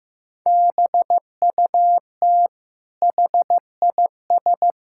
Morse code, heard as a keyed tone, about 15 wpm, 700 Hz. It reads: BUT HIS
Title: Sentences from Top 1000 Words - Review Part 5 - 15wpm